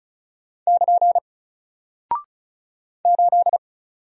Morse code, heard as a keyed tone, about 36 wpm, 700 Hz. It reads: <KN> 8
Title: Single Character (Letter:Number:Punctuation:Prosign) Alt 35wpm